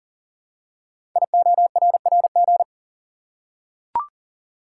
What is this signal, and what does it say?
Morse code, heard as a keyed tone, about 40 wpm, 700 Hz. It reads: IORRG
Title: Code Groups - 5 Letters - 40wpm